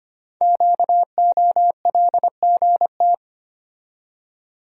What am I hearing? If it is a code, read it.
Morse code, heard as a keyed tone, about 25 wpm, 700 Hz. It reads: QOLGT
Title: Code Groups - 5 Letters - 25wpm